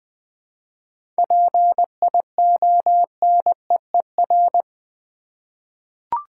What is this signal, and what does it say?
Morse code, heard as a keyed tone, about 20 wpm, 700 Hz. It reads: PIONEER
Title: 7 Letter Words - First 3 Letters Given - 20wpm